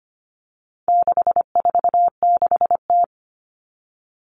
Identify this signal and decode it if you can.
Morse code, heard as a keyed tone, about 25 wpm, 700 Hz. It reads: 646T
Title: All Cut Numbers - 4-Digit 25wpm